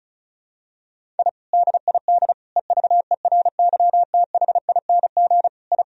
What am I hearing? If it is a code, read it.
Morse code, heard as a keyed tone, about 36 wpm, 700 Hz. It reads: I DID EVERYTHING I
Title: Sentences from Top 1000 Words - Review Part 1 - 35wpm